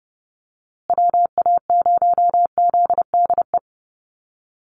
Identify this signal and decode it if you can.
Morse code, heard as a keyed tone, about 30 wpm, 700 Hz. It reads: WA0ZDE
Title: USA Call Signs Repeated 3x 30wpm